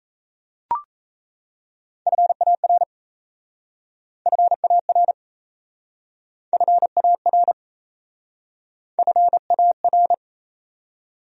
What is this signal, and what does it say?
Morse code, heard as a keyed tone, about 37 wpm, 700 Hz. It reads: FAR FAR FAR FAR